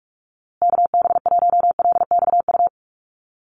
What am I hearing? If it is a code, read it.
Morse code, heard as a keyed tone, about 44 wpm, 690 Hz. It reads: KB1LXU
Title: USA Call Signs Repeated 3x 45wpm